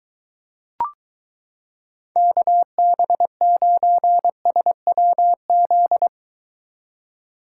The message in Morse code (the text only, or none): KB9SWZ